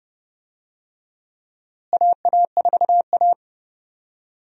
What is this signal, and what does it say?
Morse code, heard as a keyed tone, about 32 wpm, 700 Hz. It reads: AA4A
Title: USA Call Signs - 4 Characters - Repeated 2x - 30wpm